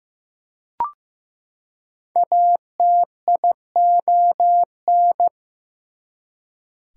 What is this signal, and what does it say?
Morse code, heard as a keyed tone, about 15 wpm, 700 Hz. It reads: ATION
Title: Prefixes and Suffixes 15wpm